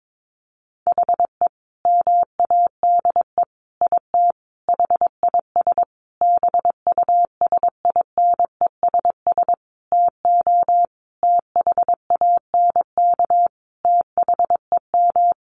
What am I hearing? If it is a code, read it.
Morse code, heard as a keyed tone, about 22 wpm, 700 Hz. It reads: HE MADE IT HIS BUSINESS TO THANK THEM
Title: Sentences from Top 1400 Words 22wpm